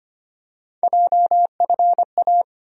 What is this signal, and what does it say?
Morse code, heard as a keyed tone, about 25 wpm, 700 Hz. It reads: JFA
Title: USA Call Sign Postfixes - 3 Letters 25wpm